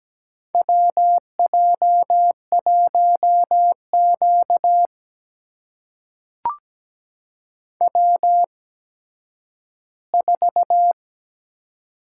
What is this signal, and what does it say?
Morse code, heard as a keyed tone, about 17 wpm, spaced as for 10 wpm, 700 Hz. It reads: WJ1Q W4
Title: USA Call Signs - 8x Character Spacing - 17wpm